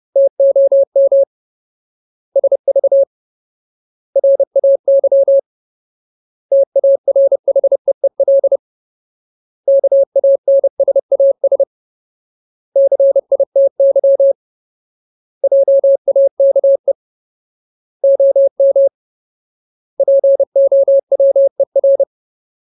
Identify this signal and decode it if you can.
Morse code, heard as a keyed tone, about 30 wpm, 550 Hz. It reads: TOM SV RAY TARHEEL KANSAS CITY JAKE OM POWER
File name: Ultimate Word List / 30WPM Ultimate